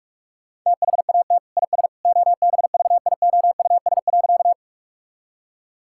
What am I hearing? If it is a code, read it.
Morse code, heard as a keyed tone, about 41 wpm, 700 Hz. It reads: THAT IS OBVIOUS.
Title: Sentences from Top 1600 Words 45wpm